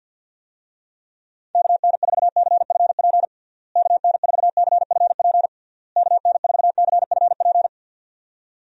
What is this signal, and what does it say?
Morse code, heard as a keyed tone, about 50 wpm, 700 Hz. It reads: KN4CFP KN4CFP KN4CFP
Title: USA Call Signs - 6 Characters - Repeated 3x - 50wpm